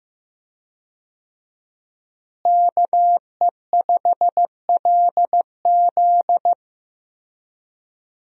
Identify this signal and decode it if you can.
Morse code, heard as a keyed tone, about 15 wpm, 700 Hz. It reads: KE5LZ
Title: USA Call Signs - 5 Characters - Repeated 3x - 15wpm